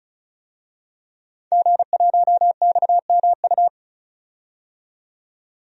Morse code, heard as a keyed tone, about 36 wpm, 700 Hz. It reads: G1XMU
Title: DXCC-100 Call Signs - 5 Characters - 35wpm